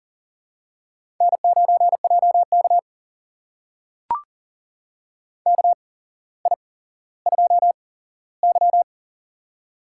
Morse code, heard as a keyed tone, about 40 wpm, spaced as for 17 wpm, 700 Hz. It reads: N9JK KI2Y
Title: USA Call Signs - 4 Characters - 8x Character Spacing - 40wpm